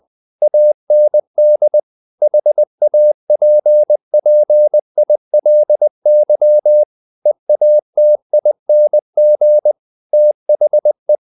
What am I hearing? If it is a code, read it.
Morse code, heard as a keyed tone, about 20 wpm, 600 Hz. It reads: AND HAPPILY EATING THE
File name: the old man and the sea / the old man and the sea part 1 20wpm